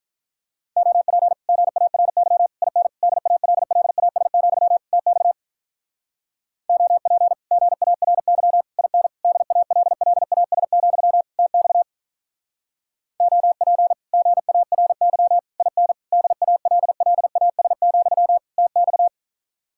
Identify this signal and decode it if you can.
Morse code, heard as a keyed tone, about 46 wpm, 700 Hz. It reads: OP GARY IN DALLAS, TX OP GARY IN DALLAS, TX OP GARY IN DALLAS, TX